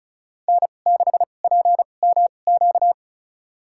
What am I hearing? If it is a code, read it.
Morse code, heard as a keyed tone, about 36 wpm, spaced as for 25 wpm, 700 Hz. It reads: N6PMQ
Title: USA Call Signs - 5 Characters - 2x Character Spacing - 35wpm